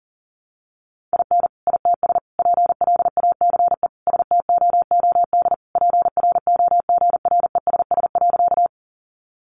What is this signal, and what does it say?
Morse code, heard as a keyed tone, about 40 wpm, 700 Hz. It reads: IN ITS PLACE STOOD PROGRESS.
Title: Sentences from Top 1400 Words 40wpm